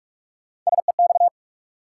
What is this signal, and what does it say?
Morse code, heard as a keyed tone, about 46 wpm, 700 Hz. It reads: SEX
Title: Sentences from Top 1600 Words - New Word Drill Alt - 45wpm